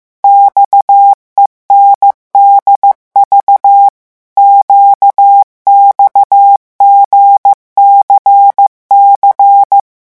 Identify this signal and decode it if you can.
Morse code, heard as a keyed tone, about 15 wpm, 800 Hz. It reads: XENDV QXGCC
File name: morse mp3 / revision7-vitesse 14 mots